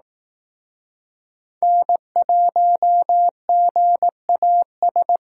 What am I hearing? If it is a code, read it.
Morse code, heard as a keyed tone, about 18 wpm, 700 Hz. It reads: N1GAS
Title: Speed Racing - USA Call Signs 15wpm